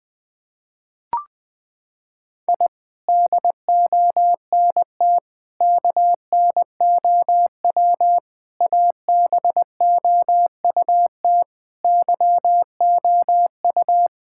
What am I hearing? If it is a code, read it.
Morse code, heard as a keyed tone, about 20 wpm, 700 Hz. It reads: I DONT KNOW ABOUT YOU
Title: Sets of 6 Words - 20wpm